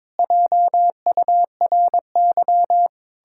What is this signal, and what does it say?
Morse code, heard as a keyed tone, about 22 wpm, 700 Hz. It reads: JURY
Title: Pangram Challenge v2 - 22wpm